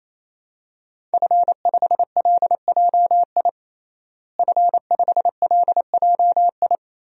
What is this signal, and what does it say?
Morse code, heard as a keyed tone, about 28 wpm, 700 Hz. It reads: F5LJI F5LJI
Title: DXCC-100 Call Signs - 5 Characters - Repeated 3x - 28wpm